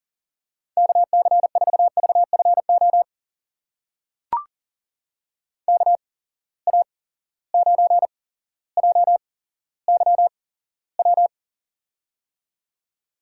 Morse code, heard as a keyed tone, about 40 wpm, spaced as for 17 wpm, 700 Hz. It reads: KC4VFO KA9JYW
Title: USA Call Signs - 6 Characters - 8x Character Spacing - 40wpm